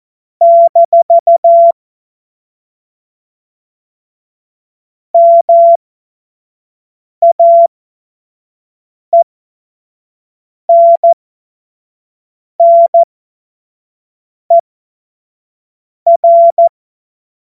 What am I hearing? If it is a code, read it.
Morse code, heard as a keyed tone, about 14 wpm, spaced as for 5 wpm, 680 Hz. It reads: - MAENNER
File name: Morsekurs / Witze / Witz 19. April 2022 25 F70